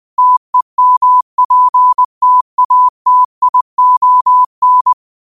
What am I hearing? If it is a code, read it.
Morse code, heard as a keyed tone, about 20 wpm, 1000 Hz. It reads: TEMPTATION
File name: qotdfiles / 20wpm